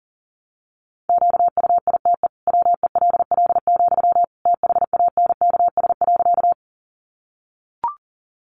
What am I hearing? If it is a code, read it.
Morse code, heard as a keyed tone, about 40 wpm, 700 Hz. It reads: QUITE WELL, THANKS.
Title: Sentences from Top 700 Words 40wpm